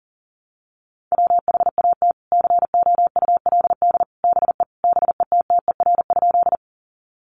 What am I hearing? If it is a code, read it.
Morse code, heard as a keyed tone, about 40 wpm, 700 Hz. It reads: WHAT COULD BE BETTER?
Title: Sentences from Top 400 Words 40wpm